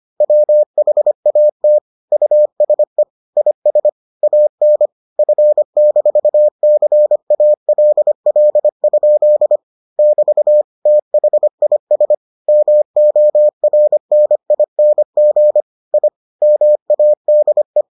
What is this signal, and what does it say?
Morse code, heard as a keyed tone, about 25 wpm, 600 Hz. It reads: WHAT USE IS AN F-CALL? <BT> THIS MORNING I MADE